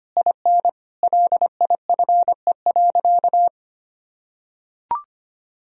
Morse code, heard as a keyed tone, about 25 wpm, 700 Hz. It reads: IN LIFE.
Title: Sentences from Top 1000 Words - Review Part 2 - 25wpm